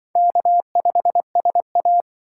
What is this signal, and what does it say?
Morse code, heard as a keyed tone, about 24 wpm, 700 Hz. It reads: K5SA
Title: Speed Racing - USA 1x2 Call Signs 20wpm